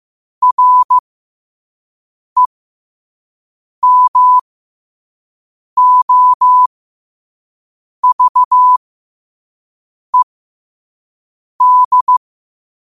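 Morse code, heard as a keyed tone, about 15 wpm, spaced as for 5 wpm, 1000 Hz. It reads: REMOVED
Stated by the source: Morse code quote of the day sent at 5 wpm.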